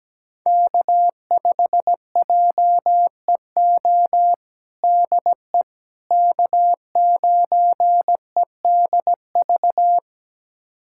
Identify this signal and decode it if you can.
Morse code, heard as a keyed tone, about 17 wpm, 700 Hz. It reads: K5JEO DE K9EDV
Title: USA Callsign DE Callsign - 5 Characters - 17wpm